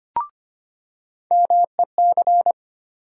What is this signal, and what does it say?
Morse code, heard as a keyed tone, about 25 wpm, 700 Hz. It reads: MEC
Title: USA Call Sign Postfixes - 3 Letters 25wpm